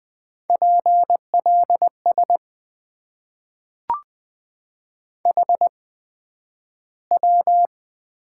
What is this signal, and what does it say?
Morse code, heard as a keyed tone, about 20 wpm, spaced as for 11 wpm, 700 Hz. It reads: PLS HW
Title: Top 90 QSO Elements - 8x Character Spacing - 20wpm